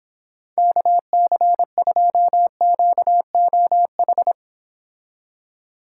Morse code, heard as a keyed tone, about 31 wpm, 700 Hz. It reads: KC2QOH